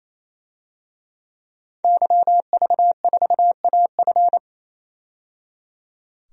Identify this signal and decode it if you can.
Morse code, heard as a keyed tone, about 27 wpm, 700 Hz. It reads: YV4AF
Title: DXCC-100 Call Signs - 5 Characters - 28wpm